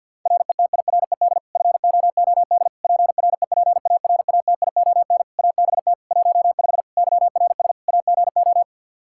Morse code, heard as a keyed tone, about 52 wpm, 700 Hz. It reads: RETIRED FOOD PREPARATION ABT 15 YRS AGO
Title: K1USN SST QSO Practice 50wpm